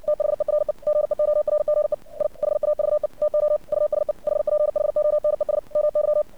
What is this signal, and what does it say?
Morse code, heard as a keyed tone, about 57 wpm, 600 Hz. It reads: THERE GEORGE I HAVE TO USE HPHONES MY
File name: archived / CRSnet